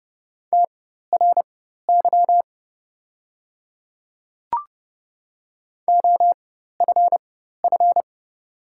Morse code, heard as a keyed tone, about 30 wpm, spaced as for 14 wpm, 690 Hz. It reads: TRY OFF
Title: 4x Character Spacing - Top 500 Words - 30wpm